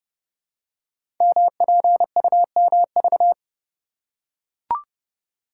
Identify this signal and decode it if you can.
Morse code, heard as a keyed tone, about 30 wpm, 700 Hz. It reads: MPUMV
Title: Code Groups - 5 Letters - 30wpm